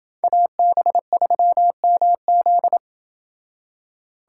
Morse code, heard as a keyed tone, about 28 wpm, 700 Hz. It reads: AB3MZ